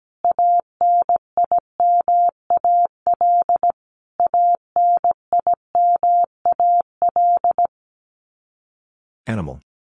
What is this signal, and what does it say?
Morse code, heard as a keyed tone, about 17 wpm, 700 Hz. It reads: ANIMAL ANIMAL
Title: Top 200 Words Repeated 17wpm